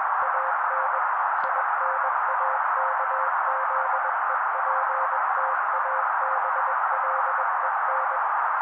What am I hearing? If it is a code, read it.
Morse code, heard as a keyed tone, about 21 wpm, 530 Hz. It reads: AN INAKZEPTABLEN